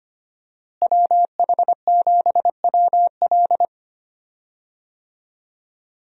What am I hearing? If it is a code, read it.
Morse code, heard as a keyed tone, about 25 wpm, 700 Hz. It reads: WH7WL
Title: USA Call Signs - 5 Characters - Repeated 3x - 25wpm